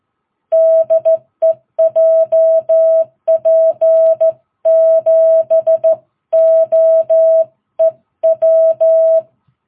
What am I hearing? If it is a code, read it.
Morse code, heard as a keyed tone, about 13 wpm, 660 Hz. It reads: DE JP7OEW
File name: jh8cbh / 2022cw